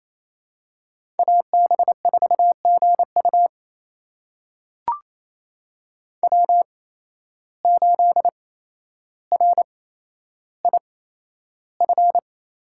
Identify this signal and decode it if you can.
Morse code, heard as a keyed tone, about 28 wpm, spaced as for 12 wpm, 700 Hz. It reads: AB4GU W8RIF
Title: USA Call Signs - 5 Characters - 8x Character Spacing - 28wpm